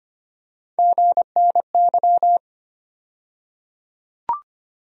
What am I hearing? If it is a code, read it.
Morse code, heard as a keyed tone, about 25 wpm, 700 Hz. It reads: GNY
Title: USA Call Sign Postfixes - 3 Letters 25wpm